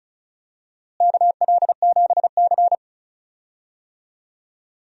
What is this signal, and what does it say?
Morse code, heard as a keyed tone, about 34 wpm, 700 Hz. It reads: KL7C
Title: DXCC-100 Call Signs - 4 Characters - Repeated 2x - 35wpm